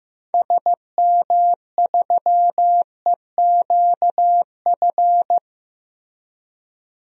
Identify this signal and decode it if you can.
Morse code, heard as a keyed tone, about 15 wpm, 700 Hz. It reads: SM3EQF
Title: DXCC-100 Call Signs - 6 Characters - Repeated 3x - 15wpm